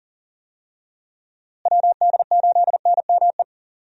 Morse code, heard as a keyed tone, about 40 wpm, 700 Hz. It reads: WD8NME
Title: USA Call Signs - 6 Characters - Repeated 2x - 40wpm